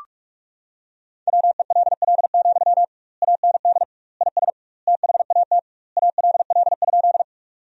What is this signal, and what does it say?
Morse code, heard as a keyed tone, about 46 wpm, 700 Hz. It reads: WELL, AND IS THAT ALL?
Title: 2x Word-Spacing - Sentences from Top 200 Words - 45wpm